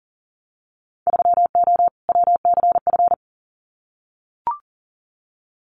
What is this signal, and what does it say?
Morse code, heard as a keyed tone, about 40 wpm, 700 Hz. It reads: 3O WCF
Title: Winter Field Day Exchanges 40wpm